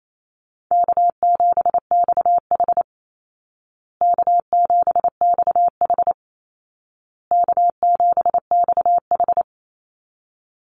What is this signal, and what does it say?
Morse code, heard as a keyed tone, about 28 wpm, 700 Hz. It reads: K7XH K7XH K7XH
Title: Easy Call Signs Repeated 3x - 28wpm